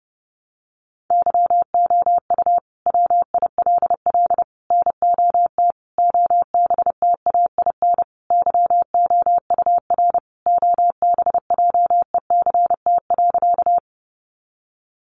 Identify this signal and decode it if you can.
Morse code, heard as a keyed tone, about 30 wpm, 700 Hz. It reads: YOU WILL NOT OBTAIN YOUR OBJECT.
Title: Sentences from Top 1300 Words 30wpm